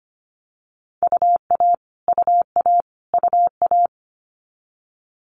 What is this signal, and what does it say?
Morse code, heard as a keyed tone, about 25 wpm, 700 Hz. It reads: UA UA UA
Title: Multi-letter Phonemes Repeated 3x 25wpm